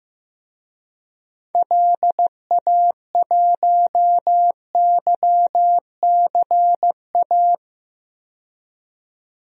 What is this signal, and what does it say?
Morse code, heard as a keyed tone, about 15 wpm, 700 Hz. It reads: LA1YCA
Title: DXCC-100 Call Signs - 6 Characters - 15wpm